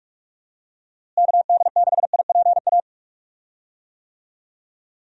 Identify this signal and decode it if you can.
Morse code, heard as a keyed tone, about 44 wpm, 700 Hz. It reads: KD6IPA
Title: USA Call Signs - 6 Characters - Repeated 2x - 45wpm